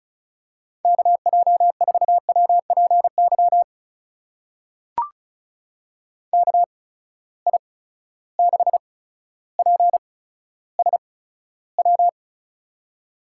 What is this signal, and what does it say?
Morse code, heard as a keyed tone, about 35 wpm, spaced as for 15 wpm, 700 Hz. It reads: KJ4WPY KI6PSW
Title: USA Call Signs - 6 Characters - 8x Character Spacing - 35wpm